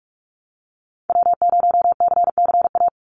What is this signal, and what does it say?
Morse code, heard as a keyed tone, about 45 wpm, 700 Hz. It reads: W0CCA
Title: NA SOTA Activator Callsigns 45wpm